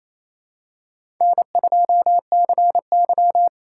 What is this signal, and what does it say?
Morse code, heard as a keyed tone, about 28 wpm, 700 Hz. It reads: N2CY
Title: USA Callsign DE Callsign - 4 Characters - 28wpm